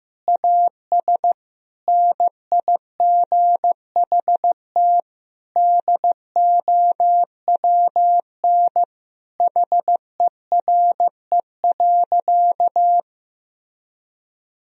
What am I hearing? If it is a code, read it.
Morse code, heard as a keyed tone, about 15 wpm, 700 Hz. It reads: AS NIGHT DOWN HERE.
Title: Sentences from Top 1000 Words - Review Part 2 - 15wpm